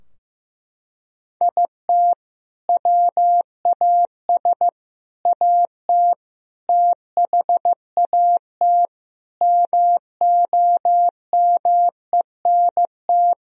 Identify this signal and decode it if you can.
Morse code, heard as a keyed tone, about 15 wpm, 700 Hz. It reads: IT WAS AT THAT MOMENT
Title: Sentences from Top 1000 Words - Review Part 3 - 15wpm